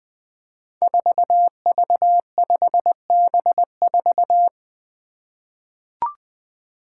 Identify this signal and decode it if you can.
Morse code, heard as a keyed tone, about 20 wpm, 700 Hz. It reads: 4V5B4
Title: Five Challenging Letter-Number Combinations 20wpm